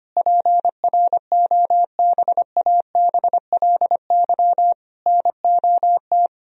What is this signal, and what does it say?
Morse code, heard as a keyed tone, about 25 wpm, 700 Hz. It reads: PROBABLY NOT
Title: Sentences from Top 1000 Words - Review Part 4 - 25wpm